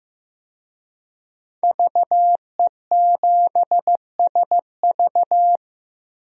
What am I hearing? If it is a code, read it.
Morse code, heard as a keyed tone, about 15 wpm, 700 Hz. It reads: VE7SV
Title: DXCC-100 Call Signs - 5 Characters - 15wpm